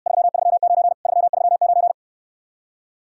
Morse code, heard as a keyed tone, about 69 wpm, 700 Hz. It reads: 339 339